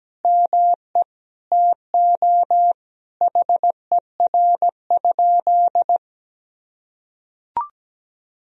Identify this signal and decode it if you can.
Morse code, heard as a keyed tone, about 17 wpm, 700 Hz. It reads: ME TO HER?
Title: Sentences from Top 200 Words and Abbreviations 17wpm